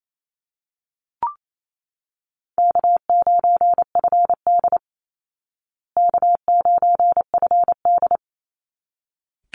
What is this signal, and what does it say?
Morse code, heard as a keyed tone, about 28 wpm, 700 Hz. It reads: K9FD K9FD
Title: Easy Call Signs Repeated 2x - 28wpm